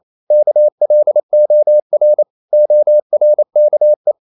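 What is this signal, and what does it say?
Morse code, heard as a keyed tone, about 28 wpm, 600 Hz. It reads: KLOR ORKE
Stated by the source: Danish short words sent in Morse at 28 wpm, from CW med Gnister.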